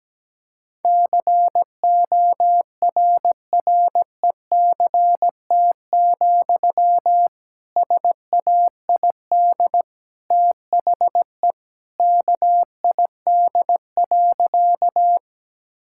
Morse code, heard as a keyed tone, about 17 wpm, 700 Hz. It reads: CORRECT, SAID THE KID.
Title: Sentences from Top 1600 Words 17wpm